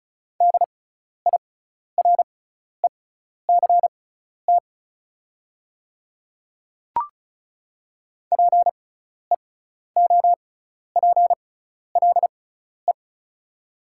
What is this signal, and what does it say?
Morse code, heard as a keyed tone, about 34 wpm, spaced as for 12 wpm, 700 Hz. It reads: DIRECT PEOPLE
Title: 6x Character Spacing - Top 500 Words - 35wpm